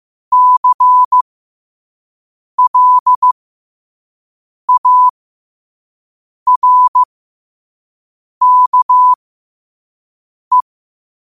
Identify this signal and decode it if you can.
Morse code, heard as a keyed tone, about 15 wpm, spaced as for 5 wpm, 1000 Hz. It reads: CLARKE
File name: qotdfiles / 5wpm